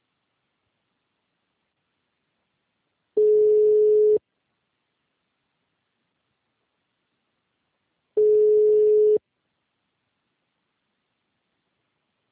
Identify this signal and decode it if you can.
Ringback tone (Europe, 425 Hz).